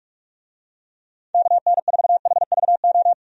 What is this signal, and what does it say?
Morse code, heard as a keyed tone, about 46 wpm, 700 Hz. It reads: KN4HVO
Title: USA Call Signs - 6 Characters - 45wpm